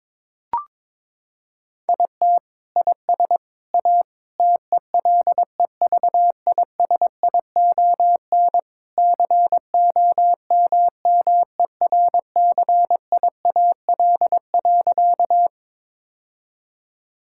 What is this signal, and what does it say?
Morse code, heard as a keyed tone, about 22 wpm, 700 Hz. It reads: IT IS A TELEVISION COMMERCIAL.
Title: Sentences from Top 1000 Words - Review Part 5 - 22wpm